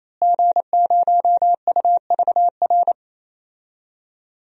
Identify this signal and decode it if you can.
Morse code, heard as a keyed tone, about 28 wpm, 700 Hz. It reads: G0UVR
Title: DXCC-100 Call Signs - 5 Characters - Repeated 3x - 28wpm